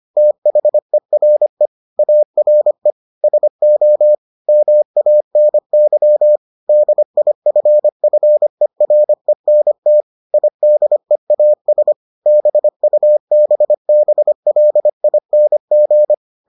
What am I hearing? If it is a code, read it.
Morse code, heard as a keyed tone, about 25 wpm, 600 Hz. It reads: THERE ARE SO MANY DIFFERENT IDEAS BUBBLING